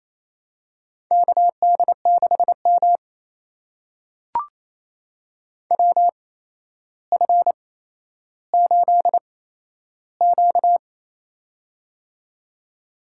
Morse code, heard as a keyed tone, about 28 wpm, spaced as for 12 wpm, 700 Hz. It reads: KD6M WF8Q
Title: USA Call Signs - 4 Characters - 8x Character Spacing - 28wpm